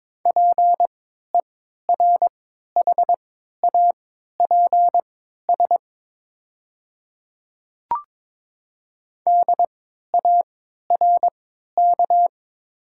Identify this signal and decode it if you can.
Morse code, heard as a keyed tone, about 22 wpm, spaced as for 12 wpm, 700 Hz. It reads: PERHAPS DARK
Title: 3x Character Spacing - Top 500 Words - 22wpm